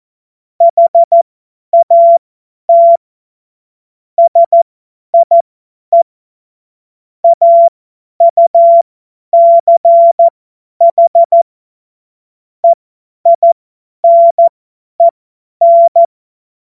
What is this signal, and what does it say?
Morse code, heard as a keyed tone, about 14 wpm, spaced as for 10 wpm, 680 Hz. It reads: HAT SIE AUCH EINEN
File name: Morsekurs / Witze / Witz 02. August 2022 50 F70